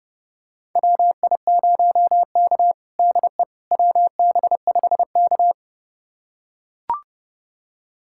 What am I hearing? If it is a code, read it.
Morse code, heard as a keyed tone, about 30 wpm, 700 Hz. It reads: WI0K DE WB5K
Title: USA Callsign DE Callsign - 4 Characters - 30wpm